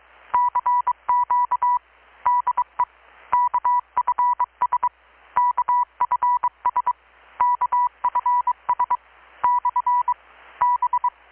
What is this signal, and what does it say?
Morse code, heard as a keyed tone, about 22 wpm, 980 Hz, with radio noise. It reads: CQ DE KFS KFS KFS / B
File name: blog / swl / 6-abril-2019-12695-USB-KPS